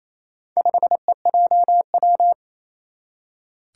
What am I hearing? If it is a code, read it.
Morse code, heard as a keyed tone, about 28 wpm, 700 Hz. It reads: 5EJW
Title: Four Challenging Letter-Number Combinations 28wpm